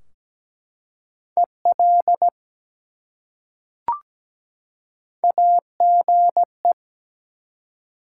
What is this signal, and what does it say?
Morse code, heard as a keyed tone, about 17 wpm, 700 Hz. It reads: EL AGE
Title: Top 180 QSO Elements 17wpm